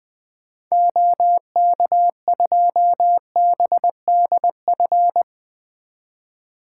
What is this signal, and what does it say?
Morse code, heard as a keyed tone, about 21 wpm, 700 Hz. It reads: OK2BDF
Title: 2023 SOTA Activator Call Signs - 20wpm